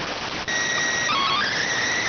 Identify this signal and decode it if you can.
SSTV (slow-scan television), PD 290.